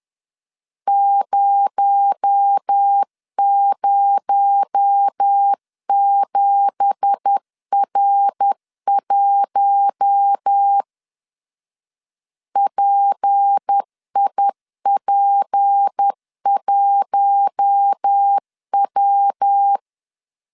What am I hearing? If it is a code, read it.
Morse code, heard as a keyed tone, about 10 wpm, 800 Hz. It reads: 007R1 PIP1W
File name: CW / DET / 1 9 1500C S B 10wpm 4439